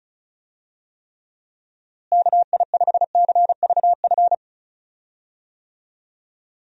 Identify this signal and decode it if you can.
Morse code, heard as a keyed tone, about 36 wpm, 700 Hz. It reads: KI5CVF
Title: USA Call Signs - 6 Characters - Repeated 2x - 35wpm